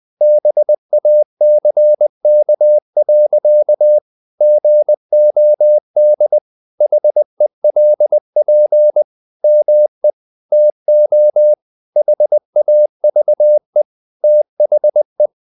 Morse code, heard as a keyed tone, about 20 wpm, 600 Hz. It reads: BACK. GOD HELP ME TO HAVE THE
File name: the old man and the sea / the old man and the sea part 1 20wpm